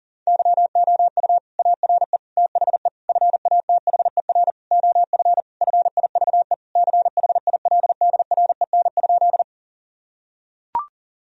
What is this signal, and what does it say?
Morse code, heard as a keyed tone, about 40 wpm, 700 Hz. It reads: YOU ARE THE FATHER OF FIVE CHILDREN?
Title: Sentences from Top 1000 Words - Review Part 2 - 40wpm